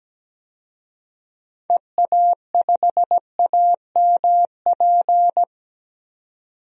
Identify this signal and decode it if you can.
Morse code, heard as a keyed tone, about 17 wpm, 700 Hz. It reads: EA5AMP
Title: DXCC-100 Call Signs - 6 Characters - 17wpm